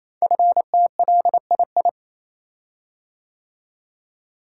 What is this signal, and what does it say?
Morse code, heard as a keyed tone, about 29 wpm, 700 Hz. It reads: FTLII
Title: Code Groups - 5 Letters - 28wpm